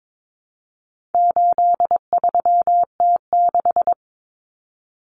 Morse code, heard as a keyed tone, about 22 wpm, 700 Hz. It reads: 83T6
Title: Common Cut Numbers - 4-Digit 22wpm